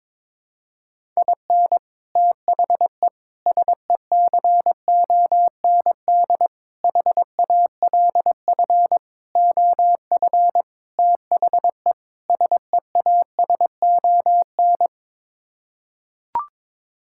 Morse code, heard as a keyed tone, about 22 wpm, 700 Hz. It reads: IN THE SECOND HALF OF THE SEASON
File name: practice / 22wpm / Sets of 7 Words - 22wpm